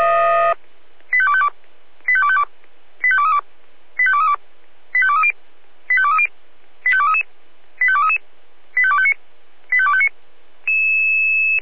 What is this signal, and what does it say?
Five-tone selective call (ZVEI), heard as quick runs of five tones, tones of about 70 ms. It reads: 85252 85252 85232 85232 85230 85230 85233 85233 85259 85259